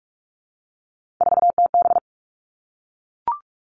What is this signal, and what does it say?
Morse code, heard as a keyed tone, about 44 wpm, 700 Hz. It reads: 4TB
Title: All Cut Numbers - 3-Digit 45wpm